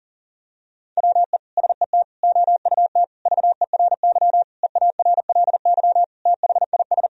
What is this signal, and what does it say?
Morse code, heard as a keyed tone, about 40 wpm, 700 Hz. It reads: WE SET OUT VERY EARLY THIS
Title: Sentences from Top 1000 Words - Review Part 2 - 40wpm